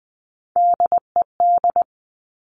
Morse code, heard as a keyed tone, about 20 wpm, 700 Hz. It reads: DED
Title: Top Three Letter Combinations in Words 20wpm